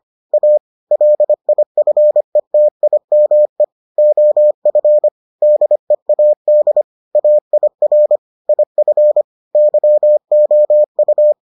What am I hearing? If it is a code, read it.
Morse code, heard as a keyed tone, about 25 wpm, 600 Hz. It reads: A LIFETIME OF DEAD AIR IF YOU